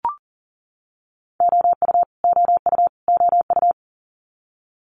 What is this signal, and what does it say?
Morse code, heard as a keyed tone, about 40 wpm, 690 Hz. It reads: OU OU OU
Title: Multi-letter Phonemes Repeated 3x 40wpm